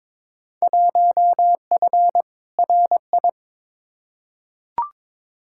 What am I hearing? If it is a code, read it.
Morse code, heard as a keyed tone, about 22 wpm, 700 Hz. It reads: 1F RI
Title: Field Day Exchanges Encore - 22wpm